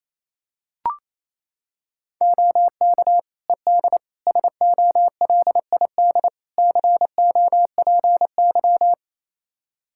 Morse code, heard as a keyed tone, about 27 wpm, 700 Hz. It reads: OK ED SOLID COPY
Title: K1USN SST QSO Practice 28wpm